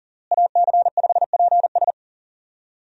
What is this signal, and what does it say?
Morse code, heard as a keyed tone, about 40 wpm, 700 Hz. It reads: AC5PS
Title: USA Call Signs - 5 Characters - Repeated 3x - 40wpm